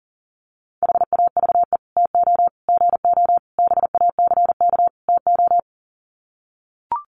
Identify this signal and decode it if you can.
Morse code, heard as a keyed tone, about 40 wpm, 700 Hz. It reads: HAVE TO GO BACK TO
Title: Sets of 5 Words Encore2 40wpm